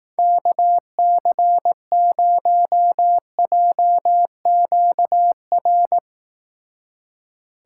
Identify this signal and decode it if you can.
Morse code, heard as a keyed tone, about 18 wpm, 700 Hz. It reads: KC0JQR
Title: Speed Racing - USA 2x3 Call Signs 15wpm